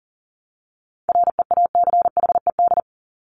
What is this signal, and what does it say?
Morse code, heard as a keyed tone, about 40 wpm, 700 Hz. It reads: REACHED
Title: ED Suffix - Top 100 Words with Base Words 40wpm